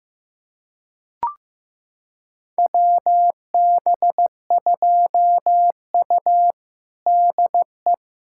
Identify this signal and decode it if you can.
Morse code, heard as a keyed tone, about 15 wpm, 700 Hz. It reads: WB2U DE
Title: USA Callsign DE Callsign - 4 Characters - 15wpm